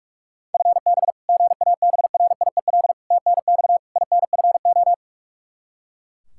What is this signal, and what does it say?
Morse code, heard as a keyed tone, about 42 wpm, 700 Hz. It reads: FB GABRIEL TNX INFO
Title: Practice QSO - Exchange 3 - More Info - 45wpm